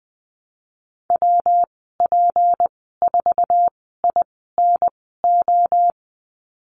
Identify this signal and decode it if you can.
Morse code, heard as a keyed tone, about 20 wpm, spaced as for 14 wpm, 700 Hz. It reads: WP4INO
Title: 167 Lesson 27g - Number 7 - Call Signs 20wpm